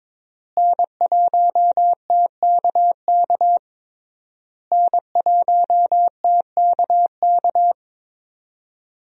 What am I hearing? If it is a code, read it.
Morse code, heard as a keyed tone, about 22 wpm, 700 Hz. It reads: N1TKK N1TKK
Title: USA Call Signs - 5 Characters - Repeated 3x - 22wpm